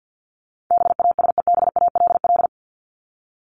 Morse code, heard as a keyed tone, about 50 wpm, 700 Hz. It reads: BASEBALL
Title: Top 1751-2000 Words 50wpm